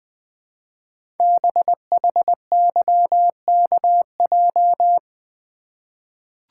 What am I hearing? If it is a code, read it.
Morse code, heard as a keyed tone, about 20 wpm, 700 Hz. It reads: BHYKJ
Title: Code Groups - 5 Letters - 20wpm